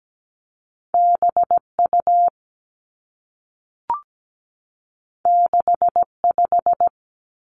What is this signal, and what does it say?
Morse code, heard as a keyed tone, about 17 wpm, 700 Hz. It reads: BU 65
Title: Two Challenging Letter-Number Combinations 17wpm